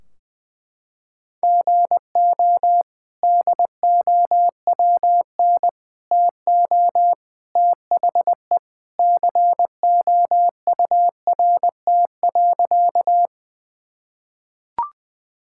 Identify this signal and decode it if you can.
Morse code, heard as a keyed tone, about 21 wpm, 700 Hz. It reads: GO DOWN TO THE COURT.
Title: Sentences from Top 1600 Words 20wpm